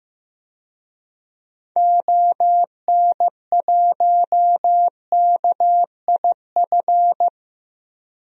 Morse code, heard as a keyed tone, about 15 wpm, 700 Hz. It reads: ON1KIF
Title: DXCC-100 Call Signs - 6 Characters - 15wpm